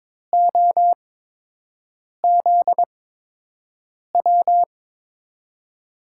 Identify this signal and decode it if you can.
Morse code, heard as a keyed tone, about 22 wpm, 700 Hz. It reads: O Z W
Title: USA Call Signs - 6 Characters - 8x Character Spacing - 22wpm